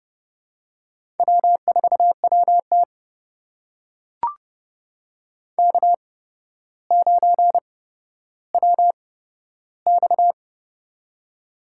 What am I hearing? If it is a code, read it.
Morse code, heard as a keyed tone, about 29 wpm, spaced as for 13 wpm, 700 Hz. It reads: W4WT K9WX
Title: USA Call Signs - 4 Characters - 8x Character Spacing - 30wpm